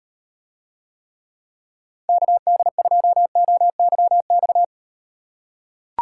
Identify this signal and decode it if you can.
Morse code, heard as a keyed tone, about 36 wpm, 700 Hz. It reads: KD2OYX